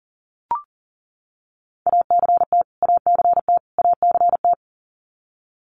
Morse code, heard as a keyed tone, about 40 wpm, 700 Hz. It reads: ACT ACT ACT
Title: Top 200 Words Repeated 40wpm